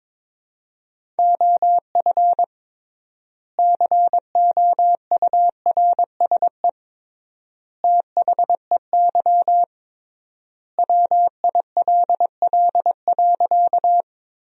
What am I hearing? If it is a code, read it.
Morse code, heard as a keyed tone, about 22 wpm, 700 Hz. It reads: OF COURSE THEY WILL.
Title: 3x Word-Spacing - Sentences from Top 500 Words - 22wpm